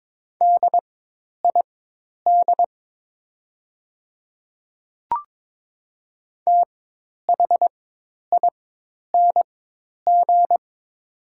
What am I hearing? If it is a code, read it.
Morse code, heard as a keyed tone, about 22 wpm, spaced as for 10 wpm, 700 Hz. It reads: DID THING
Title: 4x Character Spacing - Top 200 Words - 22wpm